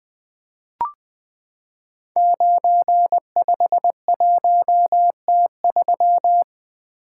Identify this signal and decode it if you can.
Morse code, heard as a keyed tone, about 20 wpm, 700 Hz. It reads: 951T3
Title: Code Groups - 5 Letter:Numbers - 20wpm